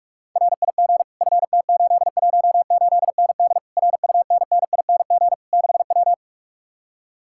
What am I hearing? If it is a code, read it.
Morse code, heard as a keyed tone, about 50 wpm, 700 Hz. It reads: RIG FT818ND RUNNING 6W